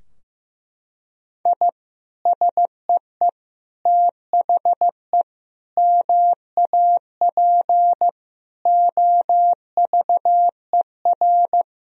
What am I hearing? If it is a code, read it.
Morse code, heard as a keyed tone, about 15 wpm, 700 Hz. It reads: I SEE THE MAP OVER
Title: Sentences from Top 1000 Words - Review Part 2 - 15wpm